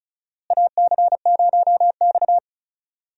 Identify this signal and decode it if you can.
Morse code, heard as a keyed tone, about 36 wpm, 700 Hz. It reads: AC0X
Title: USA Call Signs - 4 Characters - Repeated 3x - 35wpm